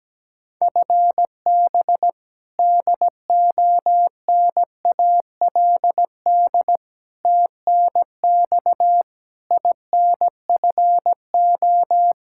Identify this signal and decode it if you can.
Morse code, heard as a keyed tone, about 17 wpm, 700 Hz. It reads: FB DONALD TNX INFO
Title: Practice QSO - Exchange 3 - More Info - 17wpm